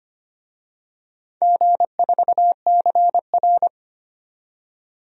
Morse code, heard as a keyed tone, about 25 wpm, 700 Hz. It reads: G4CR
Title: DXCC-100 Call Signs - 4 Characters - 25wpm